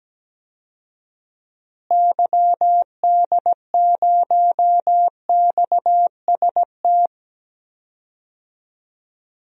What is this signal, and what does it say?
Morse code, heard as a keyed tone, about 17 wpm, 700 Hz. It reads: YD0XST
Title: DXCC-100 Call Signs - 6 Characters - Repeated 3x - 17wpm